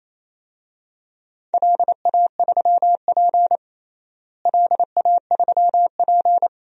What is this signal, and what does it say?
Morse code, heard as a keyed tone, about 28 wpm, 700 Hz. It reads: LA3P LA3P
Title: DXCC-100 Call Signs - 4 Characters - Repeated 2x - 28wpm